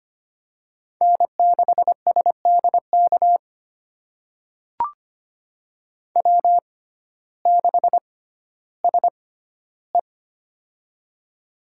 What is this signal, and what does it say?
Morse code, heard as a keyed tone, about 25 wpm, spaced as for 14 wpm, 700 Hz. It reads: N6SDK W6SE
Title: USA Call Signs - 6x Character Spacing - 25wpm